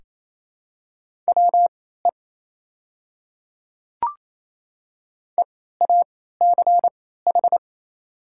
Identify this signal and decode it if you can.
Morse code, heard as a keyed tone, about 28 wpm, spaced as for 16 wpm, 700 Hz. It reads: WE EACH
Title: 3x Character Spacing - Top 200 Words - 28wpm